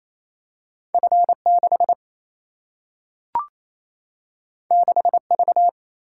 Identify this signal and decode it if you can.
Morse code, heard as a keyed tone, about 28 wpm, 700 Hz. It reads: F 6 6 V
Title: Single-Letter Single-Number 28wpm